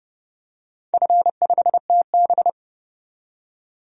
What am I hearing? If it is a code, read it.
Morse code, heard as a keyed tone, about 28 wpm, 700 Hz. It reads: F5TB
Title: DXCC-100 Call Signs - 4 Characters - 30wpm